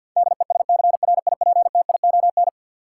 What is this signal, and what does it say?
Morse code, heard as a keyed tone, about 49 wpm, 700 Hz. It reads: DESCRIPTION
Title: Sentences from Top 1300 Words - New Word Drill Alt - 50wpm